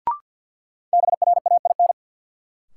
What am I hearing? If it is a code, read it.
Morse code, heard as a keyed tone, about 52 wpm, 700 Hz. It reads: BRAIN
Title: Sentences from Top 1100 Words - New Word Drill Alt - 50wpm